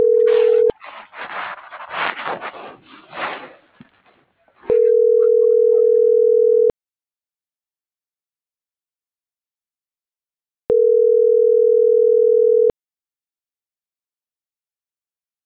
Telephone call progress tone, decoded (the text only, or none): ringback tone